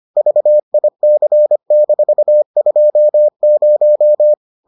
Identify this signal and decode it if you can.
Morse code, heard as a keyed tone, about 25 wpm, 600 Hz. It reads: VIC-20